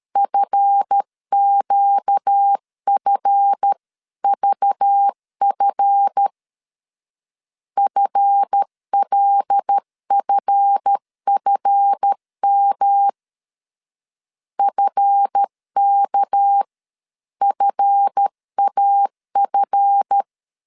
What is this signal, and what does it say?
Morse code, heard as a keyed tone, about 13 wpm, 800 Hz. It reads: FQFVF FLFFM FKFAF
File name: LEZIONI / 7 GF9Z / 7 GF 12wpm 2122